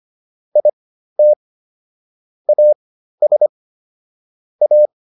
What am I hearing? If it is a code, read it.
Morse code, heard as a keyed tone, about 25 wpm, spaced as for 13 wpm, 600 Hz. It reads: IT AS A
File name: WD404 13